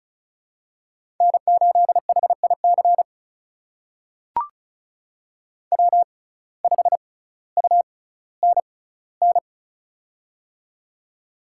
Morse code, heard as a keyed tone, about 36 wpm, spaced as for 18 wpm, 700 Hz. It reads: N8HIC W5UNN
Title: USA Call Signs - 5 Characters - 6x Character Spacing - 35wpm